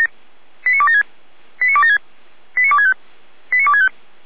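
Five-tone selective call (ZVEI), heard as quick runs of five tones, tones of about 70 ms. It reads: 89276 89276 89265 89265
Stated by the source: ZVEI five-tone selective call 89042 received on SDR.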